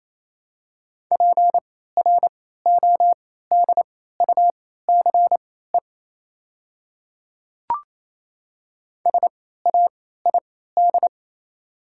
Morse code, heard as a keyed tone, about 28 wpm, spaced as for 16 wpm, 690 Hz. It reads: PRODUCE SAID
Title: 3x Character Spacing - Top 500 Words - 28wpm